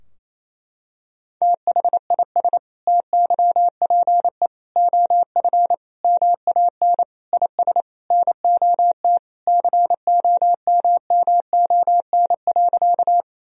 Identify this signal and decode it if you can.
Morse code, heard as a keyed tone, about 28 wpm, 700 Hz. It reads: THIS TYPE OF MAN IS NOT COMMON.
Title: Sentences from Top 1000 Words - Review Part 3 - 28wpm